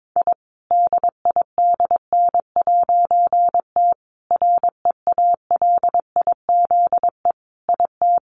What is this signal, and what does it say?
Morse code, heard as a keyed tone, about 22 wpm, 700 Hz. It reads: I DIDN'T REALIZE IT
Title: A Sound Of Thunder 22wpm